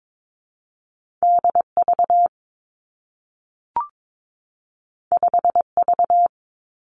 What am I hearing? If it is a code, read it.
Morse code, heard as a keyed tone, about 22 wpm, 700 Hz. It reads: D V 5 V
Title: Two Challenging Letter-Number Combinations 22wpm